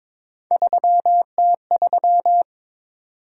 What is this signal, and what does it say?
Morse code, heard as a keyed tone, about 22 wpm, 700 Hz. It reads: 3T3
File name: practice / 22wpm / Three Letter-Number Combinations - 22wpm